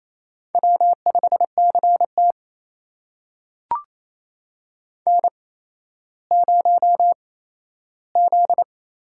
Morse code, heard as a keyed tone, about 28 wpm, 700 Hz. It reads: W 5 C T N 0 Z
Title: USA Call Signs - 4 Characters - 8x Character Spacing - 28wpm